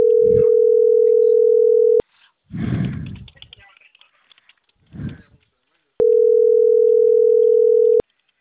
Ringback tone (North America), 440 + 480 Hz.